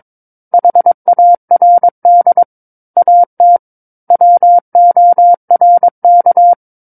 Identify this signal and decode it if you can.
Morse code, heard as a keyed tone, about 22 wpm, 700 Hz. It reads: HARD AT WORK
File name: Book One / 13-Chapter 13 - 22 WPM